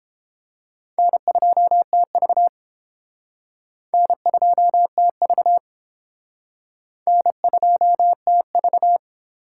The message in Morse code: N2TV N2TV N2TV

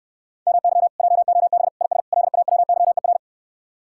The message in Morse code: MY JOB IS LAWYER

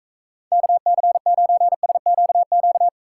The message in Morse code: KC9SQQ

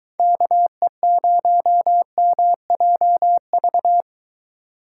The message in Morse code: KE0MJV

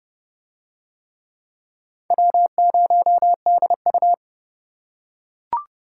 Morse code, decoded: W0DU

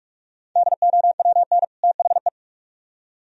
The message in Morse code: DOWN THE

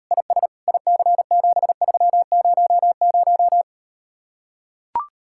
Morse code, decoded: IS IC7300

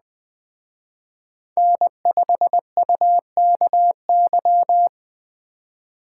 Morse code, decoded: N5UKY

